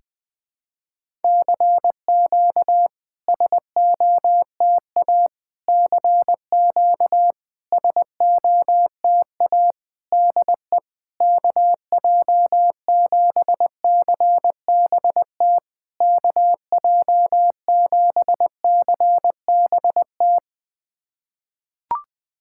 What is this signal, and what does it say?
Morse code, heard as a keyed tone, about 20 wpm, 700 Hz. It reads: CQ SOTA CQ SOTA DE KJ7CBT KJ7CBT
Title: SOTA QSO Practice 20wpm